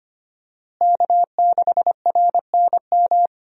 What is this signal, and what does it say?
Morse code, heard as a keyed tone, about 25 wpm, 700 Hz. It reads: K6RNM
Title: USA Call Signs - 5 Characters - Repeated 3x - 25wpm